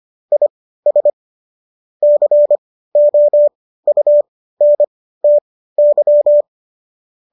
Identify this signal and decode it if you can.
Morse code, heard as a keyed tone, about 25 wpm, spaced as for 15 wpm, 590 Hz. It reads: IS COUNTY